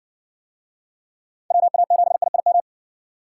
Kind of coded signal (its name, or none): Morse code